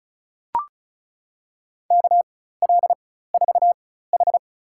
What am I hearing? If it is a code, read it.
Morse code, heard as a keyed tone, about 36 wpm, spaced as for 16 wpm, 700 Hz. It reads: KL4H
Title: USA Call Signs - 4 Characters - 4x Character Spacing - 35wpm